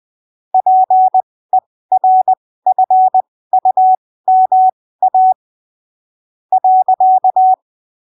Morse code, heard as a keyed tone, about 20 wpm, spaced as for 15 wpm, 750 Hz. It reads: PERFUMA .
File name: arquivos / texto / texto 20 20ppm